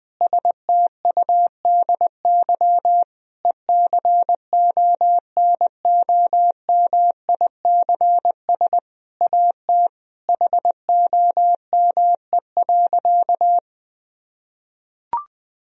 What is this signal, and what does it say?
Morse code, heard as a keyed tone, about 20 wpm, 700 Hz. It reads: STUDY ECONOMICS AT HOME.
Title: Sentences from Top 1000 Words - Review Part 3 - 20wpm